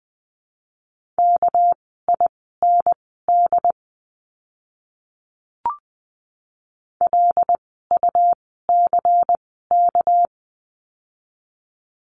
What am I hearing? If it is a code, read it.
Morse code, heard as a keyed tone, about 20 wpm, spaced as for 14 wpm, 700 Hz. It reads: KIND LUCK
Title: 199 Lesson 32f - Letter k - Words 20wpm